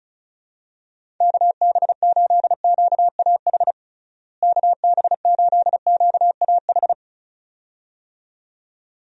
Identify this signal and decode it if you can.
Morse code, heard as a keyed tone, about 36 wpm, 700 Hz. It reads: KB8QAH KB8QAH
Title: USA Call Signs - 6 Characters - Repeated 2x - 35wpm